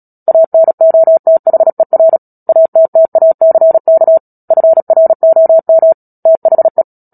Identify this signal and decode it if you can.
Morse code, heard as a keyed tone, about 37 wpm, 650 Hz. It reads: ANOTHER ATTACK FROM THE
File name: Book Two / 10-Chapter 10 - Epilogue - 36 WPM